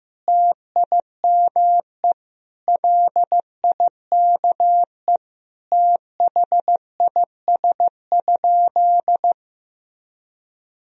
Morse code, heard as a keyed tone, about 15 wpm, 700 Hz. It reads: TIME LIKE THIS?
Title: Sentences from Top 1000 Words - Review Part 2 - 15wpm